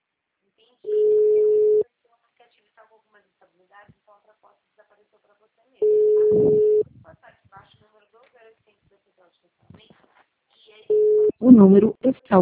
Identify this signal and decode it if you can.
Ringback tone (Europe, 425 Hz), with speech.
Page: Campinas - SP